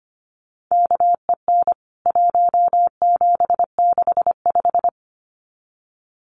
Morse code, heard as a keyed tone, about 25 wpm, 700 Hz. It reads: KEN 1765
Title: CWT Exchange 25wpm